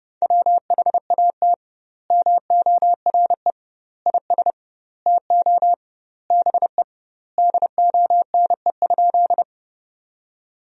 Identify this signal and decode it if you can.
Morse code, heard as a keyed tone, about 30 wpm, 700 Hz. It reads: WHAT MORE IS TO BE DONE?
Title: 2x Word-Spacing - Sentences from Top 500 Words - 30wpm